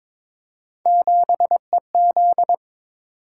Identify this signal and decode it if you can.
Morse code, heard as a keyed tone, about 22 wpm, 700 Hz. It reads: 7EZ
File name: practice / 22wpm / Three Letter-Number Combinations - 22wpm